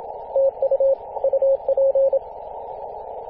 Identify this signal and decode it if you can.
Morse code, heard as a keyed tone, about 26 wpm, 540 Hz. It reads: TU UP
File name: posts / 2016-arp-15-2115z-vk9l-g7vjr-18-081mhz-cw